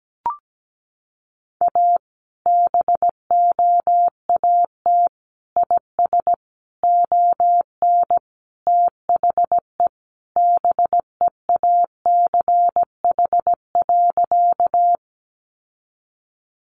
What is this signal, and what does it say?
Morse code, heard as a keyed tone, about 17 wpm, 700 Hz. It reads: A BOAT IS ON THE BEACH.
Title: Sentences from Top 1500 Words 17wpm